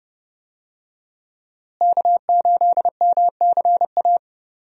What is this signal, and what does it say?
Morse code, heard as a keyed tone, about 29 wpm, 700 Hz. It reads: K8MCA
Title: USA Call Signs - 5 Characters - Repeated 2x - 30wpm